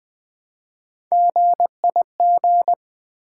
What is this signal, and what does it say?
Morse code, heard as a keyed tone, about 20 wpm, 700 Hz. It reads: GIG
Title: Build A Word - 1 Letter At A Time - 3-Letter Words - 20wpm